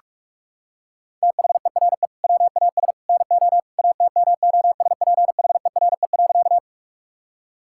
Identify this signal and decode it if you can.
Morse code, heard as a keyed tone, about 46 wpm, 700 Hz. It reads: THERE WAS NO ATMOSPHERE.
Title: Sentences from Top 1600 Words 45wpm